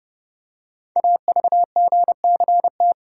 Morse code, heard as a keyed tone, about 30 wpm, 700 Hz. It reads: AVGCT
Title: Code Groups - 5 Letter:Numbers - 30wpm